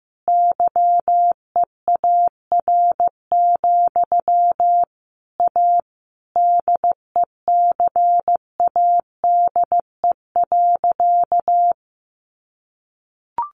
Morse code, heard as a keyed tone, about 15 wpm, 700 Hz. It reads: YEAR, A DECADE.
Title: A Sound Of Thunder 15wpm